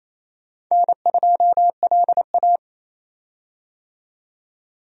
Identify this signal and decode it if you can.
Morse code, heard as a keyed tone, about 28 wpm, 700 Hz. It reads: N2LA
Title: USA Call Signs - 4 Characters - 28wpm